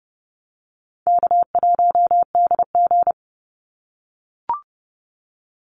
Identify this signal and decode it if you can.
Morse code, heard as a keyed tone, about 30 wpm, 700 Hz. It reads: K1DG
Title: Easy Call Signs - 30wpm